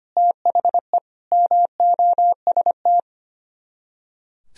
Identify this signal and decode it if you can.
Morse code, heard as a keyed tone, about 31 wpm, 700 Hz. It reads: THE MOST
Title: Speed-Racing - Sets of 4 Words 25wpm